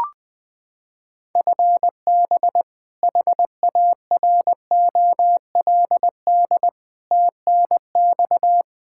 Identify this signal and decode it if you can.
Morse code, heard as a keyed tone, about 20 wpm, 700 Hz. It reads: FB HAROLD TNX
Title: Practice QSO - Exchange 3 - More Info - 20wpm